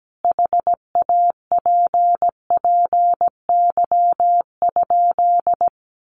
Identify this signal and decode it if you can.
Morse code, heard as a keyed tone, about 17 wpm, 700 Hz. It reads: HAPPY?
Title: Sentences from Top 1000 Words 17wpm